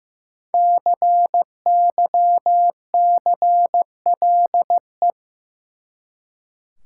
Morse code, heard as a keyed tone, about 15 wpm, 700 Hz. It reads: CYCLE
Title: Five Letter Words - 15wpm